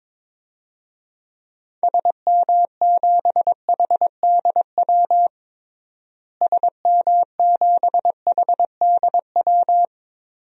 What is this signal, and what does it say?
Morse code, heard as a keyed tone, about 22 wpm, 700 Hz. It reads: SM7HDW SM7HDW
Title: DXCC-100 Call Signs - 6 Characters - Repeated 2x - 22wpm